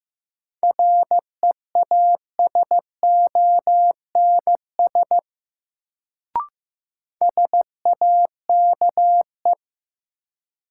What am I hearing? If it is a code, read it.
Morse code, heard as a keyed tone, about 15 wpm, 700 Hz. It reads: REASONS SAKE
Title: Sets of 4 Words - New Word Drill - 15wpm